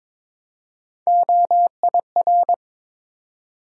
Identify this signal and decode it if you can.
Morse code, heard as a keyed tone, about 22 wpm, 700 Hz. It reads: OIR
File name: practice / 22wpm / Three Letter-Number Combinations - 22wpm